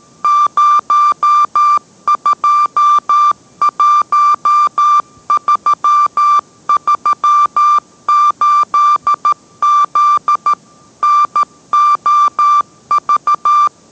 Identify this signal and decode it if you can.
Morse code, heard as a keyed tone, about 13 wpm, 1180 Hz. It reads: 021338Z NOV